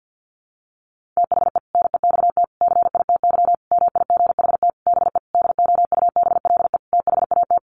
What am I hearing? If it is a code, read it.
Morse code, heard as a keyed tone, about 50 wpm, 700 Hz. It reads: THE NEXT CITY MIGHT BE DOUBLE THAT
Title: Sentences from Top 1400 Words 50wpm